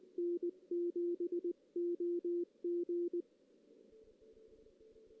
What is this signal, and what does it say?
Morse code, heard as a keyed tone, about 19 wpm, 350 Hz. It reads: N7OG
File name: 2016 JAN CW NAQP / 2016 Jan naqp k5ot - 0-3